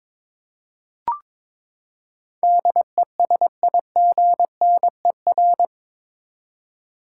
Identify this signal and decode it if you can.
Morse code, heard as a keyed tone, about 22 wpm, 700 Hz. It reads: DESIGNER
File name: practice / 22wpm / Eight Letter Words - 22wpm